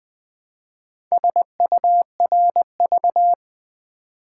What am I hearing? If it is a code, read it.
Morse code, heard as a keyed tone, about 20 wpm, 700 Hz. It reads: SURV
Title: Build A Word - 2 Letters At A Time - 8-Letter Words - 20wpm